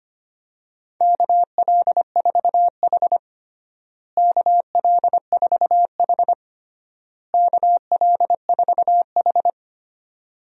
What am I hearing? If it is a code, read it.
Morse code, heard as a keyed tone, about 25 wpm, 700 Hz. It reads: KL4H KL4H KL4H
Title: USA Call Signs - 4 Characters - Repeated 3x - 25wpm